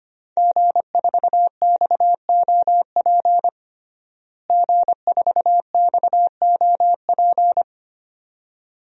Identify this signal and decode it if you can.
Morse code, heard as a keyed tone, about 25 wpm, 700 Hz. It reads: G4XOP G4XOP
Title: DXCC-100 Call Signs - 5 Characters - Repeated 2x - 25wpm